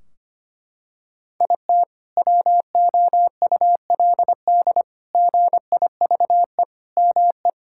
Morse code, heard as a keyed tone, about 25 wpm, 700 Hz. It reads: IT WOULD GIVE ME
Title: Sentences from Top 1600 Words 25wpm